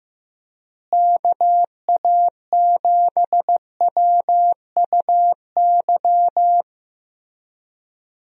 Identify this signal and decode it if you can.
Morse code, heard as a keyed tone, about 15 wpm, 700 Hz. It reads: KA7WUY
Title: USA Call Signs - 6 Characters - Repeated 2x - 15wpm